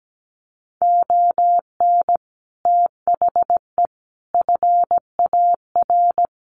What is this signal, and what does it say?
Morse code, heard as a keyed tone, about 17 wpm, 700 Hz. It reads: ON THE FAR
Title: Sets of 5 Words 17wpm